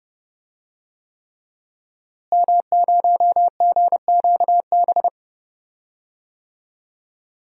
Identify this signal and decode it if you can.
Morse code, heard as a keyed tone, about 30 wpm, 700 Hz. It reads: M0GQB
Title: DXCC-100 Call Signs - 5 Characters - Repeated 3x - 30wpm